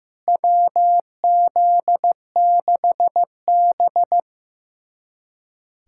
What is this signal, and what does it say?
Morse code, heard as a keyed tone, about 15 wpm, 700 Hz. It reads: WZ6B
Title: USA Callsign DE Callsign - 4 Characters - 15wpm